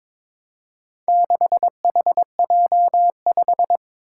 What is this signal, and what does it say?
Morse code, heard as a keyed tone, about 22 wpm, 700 Hz. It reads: 6HJ5
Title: Four Challenging Letter-Number Combinations 22wpm